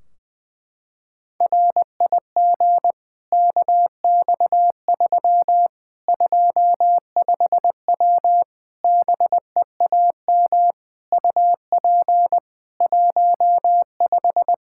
Morse code, heard as a keyed tone, about 20 wpm, 700 Hz. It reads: RIG KX3 25W BEAM UP 15
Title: K1USN SST QSO Practice 20wpm